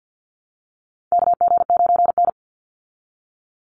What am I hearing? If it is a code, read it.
Morse code, heard as a keyed tone, about 50 wpm, 700 Hz. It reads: KG9N
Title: Call Signs - 50wpm